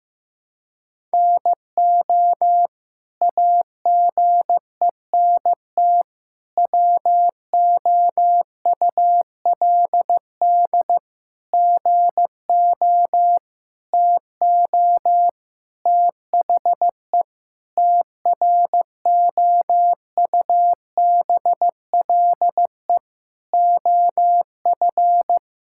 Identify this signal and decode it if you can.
Morse code, heard as a keyed tone, about 15 wpm, 700 Hz. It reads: NO AGENT WOULD GO TO THE TROUBLE OF
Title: Sentences from Top 1600 Words 15wpm